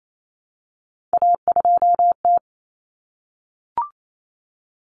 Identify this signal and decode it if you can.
Morse code, heard as a keyed tone, about 28 wpm, 700 Hz. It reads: A2T
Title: Common Cut Numbers - 3-Digit 28wpm